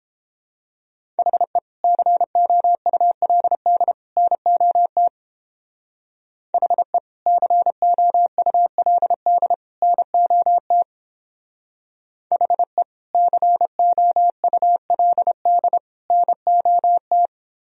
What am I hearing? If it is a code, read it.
Morse code, heard as a keyed tone, about 29 wpm, 700 Hz. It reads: HE COULD NOT HE COULD NOT HE COULD NOT